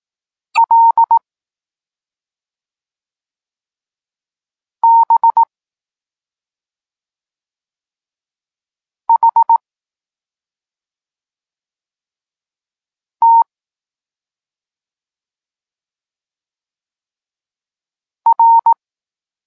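Morse code, heard as a keyed tone, about 18 wpm, 920 Hz. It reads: L B H T R
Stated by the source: Morse code lesson for learning the letters K and X, from the SP9MOA Niepołomicki Klub Krótkofalowców course.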